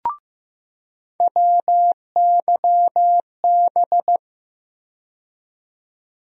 Morse code, heard as a keyed tone, about 15 wpm, 700 Hz. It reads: WYB
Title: USA Call Sign Postfixes - 3 Letters 15wpm